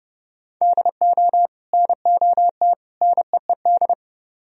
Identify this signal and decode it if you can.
Morse code, heard as a keyed tone, about 37 wpm, 700 Hz. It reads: DO NOT NEED
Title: Speed-Racing - Sets of 3 Words 30wpm